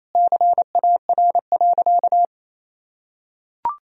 Morse code, heard as a keyed tone, about 28 wpm, 700 Hz. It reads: CAR.
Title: Sentences from Top 1000 Words - Review Part 1 - 28wpm